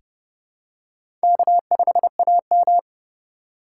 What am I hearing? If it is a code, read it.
Morse code, heard as a keyed tone, about 30 wpm, 700 Hz. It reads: K5AM
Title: K1USN SST QSO Practice - 30wpm